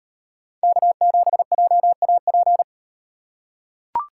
Morse code, heard as a keyed tone, about 38 wpm, 700 Hz. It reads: K7JAP